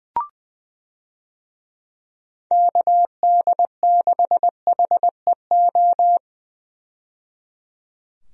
Morse code, heard as a keyed tone, about 20 wpm, 700 Hz. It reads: KD6HEO
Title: USA Call Signs - 6 Characters - 20wpm